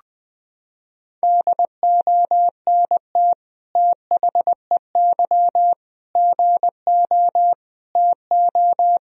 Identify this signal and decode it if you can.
Morse code, heard as a keyed tone, about 21 wpm, 700 Hz. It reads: DONT THEY GO TO
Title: Sentences from Top 1000 Words - Review Part 1 - 20wpm